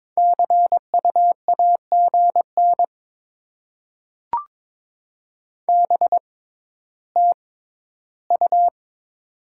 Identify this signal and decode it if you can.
Morse code, heard as a keyed tone, about 23 wpm, spaced as for 13 wpm, 690 Hz. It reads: CU AGN BTU
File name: Top 20 QSO Elements - 6x Character Spacing - 22wpm